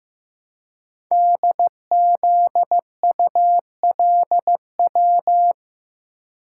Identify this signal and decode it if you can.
Morse code, heard as a keyed tone, about 15 wpm, 700 Hz. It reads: DZULW
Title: Code Groups - 5 Letters - 15wpm